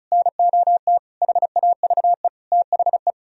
Morse code, heard as a keyed tone, about 44 wpm, 700 Hz. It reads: NOT HAVE THE